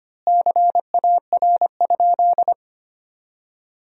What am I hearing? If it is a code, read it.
Morse code, heard as a keyed tone, about 25 wpm, 700 Hz. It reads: CAR?
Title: 3x Word-Spacing - Sentences from Top 300 Words - 25wpm